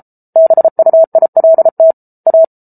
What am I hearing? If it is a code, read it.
Morse code, heard as a keyed tone, about 34 wpm, 650 Hz. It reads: BUILT A